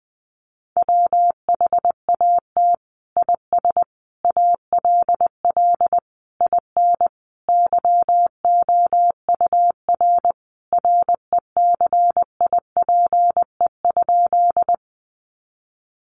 Morse code, heard as a keyed tone, about 20 wpm, 700 Hz. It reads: WHAT IS ALL IN YOUR RECIPE?
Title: Sentences from Top 1200 Words 20wpm